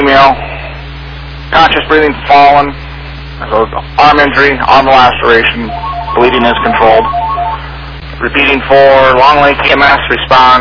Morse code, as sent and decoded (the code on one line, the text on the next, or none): none